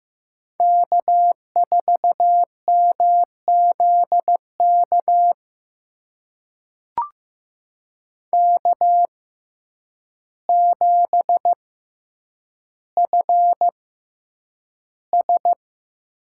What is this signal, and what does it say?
Morse code, heard as a keyed tone, about 15 wpm, spaced as for 8 wpm, 700 Hz. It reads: K4MZK K7FS
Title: USA Call Signs - 5 Characters - 6x Character Spacing - 15wpm